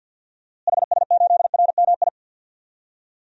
Morse code, heard as a keyed tone, about 46 wpm, 700 Hz. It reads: HS8RMI
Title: DXCC-100 Call Signs - 6 Characters - Repeated 3x - 50wpm